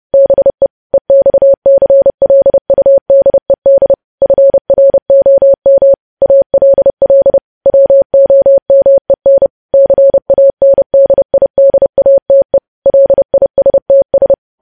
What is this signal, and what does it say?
Morse code, heard as a keyed tone, about 30 wpm, 550 Hz. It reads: BE EXCLUDED FROM ALL WOMEN CANDIDATE LISTS
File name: podcast / News headlines-1x-30x30